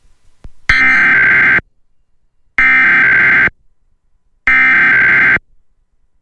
Emergency Alert System (SAME) tones (header bursts), heard as harsh buzzing bursts. It is heard from 0.7 to 5.4 s.